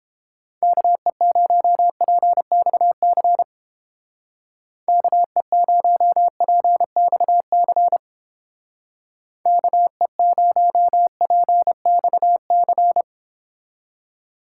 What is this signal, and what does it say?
Morse code, heard as a keyed tone, about 29 wpm, 700 Hz. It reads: KE0PXC KE0PXC KE0PXC